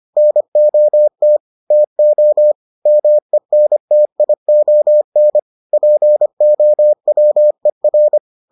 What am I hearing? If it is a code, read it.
Morse code, heard as a keyed tone, about 25 wpm, 600 Hz. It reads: NOT TO MENTION POWER